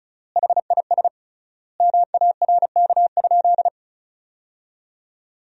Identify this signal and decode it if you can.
Morse code, heard as a keyed tone, about 34 wpm, 700 Hz. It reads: HIS MARK?
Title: 3x Word-Spacing - Sentences from Top 300 Words - 35wpm